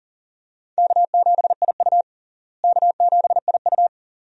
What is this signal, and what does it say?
Morse code, heard as a keyed tone, about 37 wpm, 700 Hz. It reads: K7IU K7IU
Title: USA Call Signs - 4 Characters - Repeated 2x - 40wpm